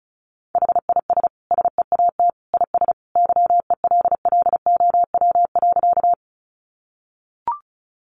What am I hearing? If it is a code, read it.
Morse code, heard as a keyed tone, about 34 wpm, 700 Hz. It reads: HIS SEAT IS YELLOW.
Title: Additional Easy Phrases 35wpm